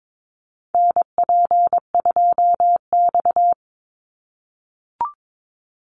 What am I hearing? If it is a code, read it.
Morse code, heard as a keyed tone, about 22 wpm, 700 Hz. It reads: NP2X
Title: Call Signs - 22wpm